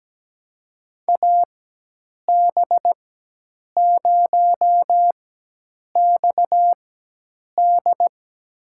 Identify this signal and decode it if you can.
Morse code, heard as a keyed tone, about 17 wpm, spaced as for 8 wpm, 700 Hz. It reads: AB0XD
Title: USA Call Signs - 5 Characters - 4x Character Spacing - 17wpm